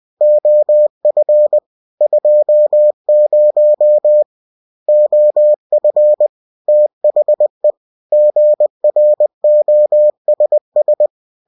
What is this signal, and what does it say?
Morse code, heard as a keyed tone, about 20 wpm, 600 Hz. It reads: OF 20 OF THE GROSS
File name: cw-books / winnie-the-pooh / winnie-20wpm-page-0117